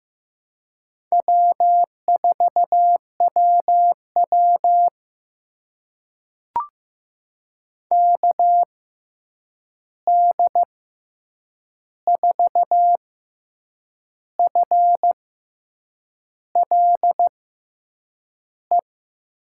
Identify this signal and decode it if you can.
Morse code, heard as a keyed tone, about 15 wpm, spaced as for 7 wpm, 700 Hz. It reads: W4WW KD4FLE
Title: USA Call Signs - 6x Character Spacing - 15wpm